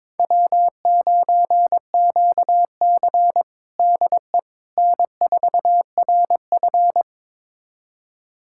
Morse code, heard as a keyed tone, about 22 wpm, 700 Hz. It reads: W9QC DE N4RF
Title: USA Callsign DE Callsign - 4 Characters - 22wpm